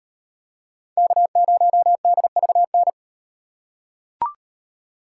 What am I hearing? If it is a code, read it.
Morse code, heard as a keyed tone, about 36 wpm, 700 Hz. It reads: K0DVN